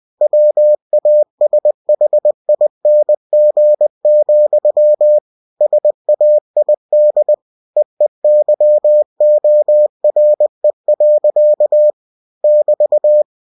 Morse code, heard as a keyed tone, about 20 wpm, 600 Hz. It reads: WASHING, SAID EEYORE. <BT>
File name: cw-books / winnie-the-pooh / winnie-20wpm-page-0085